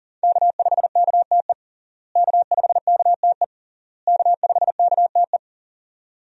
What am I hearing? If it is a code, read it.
Morse code, heard as a keyed tone, about 40 wpm, 700 Hz. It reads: K5KTE K5KTE K5KTE
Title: USA Call Signs - 5 Characters - Repeated 3x - 40wpm